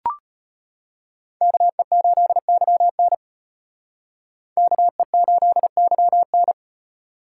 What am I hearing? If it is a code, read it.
Morse code, heard as a keyed tone, about 32 wpm, 700 Hz. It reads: KE8YN KE8YN